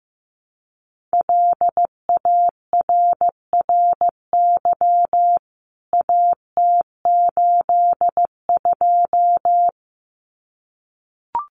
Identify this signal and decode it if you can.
Morse code, heard as a keyed tone, about 15 wpm, 700 Hz. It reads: LARRY AT82
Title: CWT Exchange 15wpm